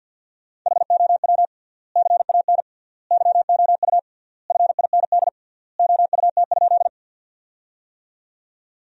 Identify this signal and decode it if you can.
Morse code, heard as a keyed tone, about 53 wpm, 700 Hz. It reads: HOW CAN YOU FIND OUT?
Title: 3x Word-Spacing - Sentences from Top 200 Words - 50wpm